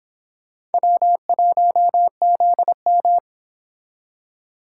Morse code, heard as a keyed tone, about 29 wpm, 700 Hz. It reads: W1ZM